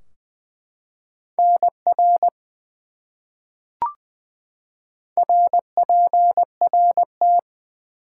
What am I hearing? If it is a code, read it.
Morse code, heard as a keyed tone, about 20 wpm, 700 Hz. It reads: NR RPRT
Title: Top 180 QSO Elements 20wpm